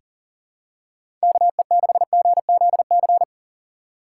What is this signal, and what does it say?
Morse code, heard as a keyed tone, about 40 wpm, 700 Hz. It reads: KE6GZC
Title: USA Call Signs - 6 Characters - 3x Character Spacing - 40wpm